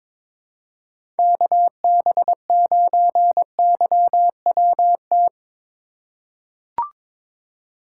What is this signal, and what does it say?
Morse code, heard as a keyed tone, about 22 wpm, 700 Hz. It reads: KB9YWT
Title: USA Call Signs - 2x Character Spacing - 22wpm